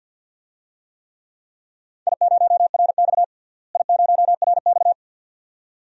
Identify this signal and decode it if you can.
Morse code, heard as a keyed tone, about 48 wpm, 700 Hz. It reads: I0RX I0RX
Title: DXCC-100 Call Signs - 4 Characters - Repeated 2x - 50wpm